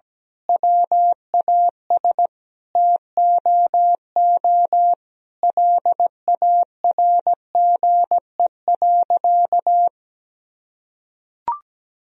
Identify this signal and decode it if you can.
Morse code, heard as a keyed tone, about 17 wpm, 700 Hz. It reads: WAS TOO LARGE.
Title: Sentences from Top 1000 Words - Review Part 1 - 17wpm